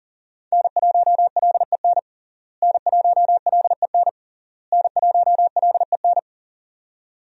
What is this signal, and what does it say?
Morse code, heard as a keyed tone, about 40 wpm, 700 Hz. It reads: N1LEN N1LEN N1LEN
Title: USA Call Signs - 5 Characters - Repeated 3x - 40wpm